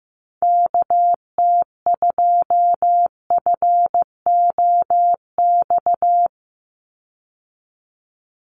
Morse code, heard as a keyed tone, about 15 wpm, 700 Hz. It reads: KT2FOX
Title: USA Call Signs 15wpm